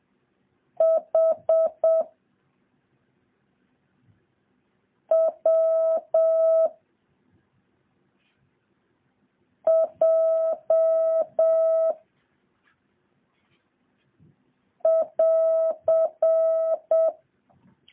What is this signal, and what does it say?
Morse code, heard as a keyed tone, about 7 wpm, 650 Hz. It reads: H W J <AR>
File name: jh8cbh / 2023cw